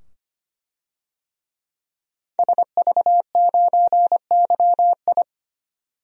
Morse code, heard as a keyed tone, about 25 wpm, 700 Hz. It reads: SV9YI
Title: DXCC-100 Call Signs - 5 Characters - 25wpm